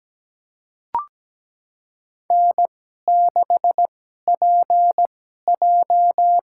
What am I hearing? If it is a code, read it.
Morse code, heard as a keyed tone, about 17 wpm, spaced as for 12 wpm, 700 Hz. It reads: N6PJ
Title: USA Call Signs - 2x Character Spacing - 17wpm